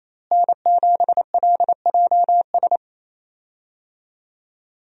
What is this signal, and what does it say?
Morse code, heard as a keyed tone, about 28 wpm, 700 Hz. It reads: N7LJS
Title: USA Call Signs - 5 Characters - Repeated 2x - 28wpm